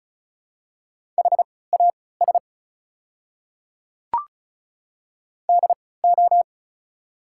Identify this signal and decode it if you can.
Morse code, heard as a keyed tone, about 36 wpm, spaced as for 20 wpm, 700 Hz. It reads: HAS DO
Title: 3x Character Spacing - Top 200 Words - 35wpm